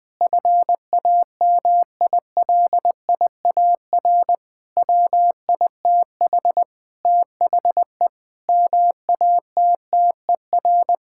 Morse code, heard as a keyed tone, about 20 wpm, 700 Hz. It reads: FAMILIAR WITH THE MATTER
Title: Sentences with 7-Word Phrases - 20wpm